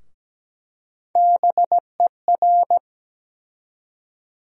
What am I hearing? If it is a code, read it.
Morse code, heard as a keyed tone, about 17 wpm, 700 Hz. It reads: BER
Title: USA Call Sign Postfixes - 3 Letters 17wpm